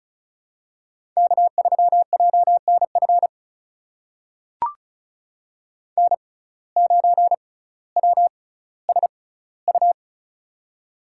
Morse code, heard as a keyed tone, about 36 wpm, spaced as for 18 wpm, 700 Hz. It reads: K3JNF N9WSU
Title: USA Call Signs - 5 Characters - 6x Character Spacing - 35wpm